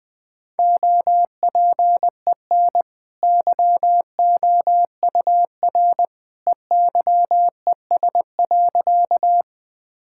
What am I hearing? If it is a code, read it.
Morse code, heard as a keyed tone, about 20 wpm, 700 Hz. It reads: OPEN YOUR EYES.
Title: Easy Sentences with Hints - 20wpm